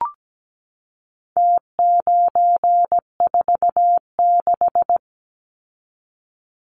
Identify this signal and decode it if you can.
Morse code, heard as a keyed tone, about 17 wpm, 700 Hz. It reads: T946
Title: Numbers - Contest Style - 4-Digit 17wpm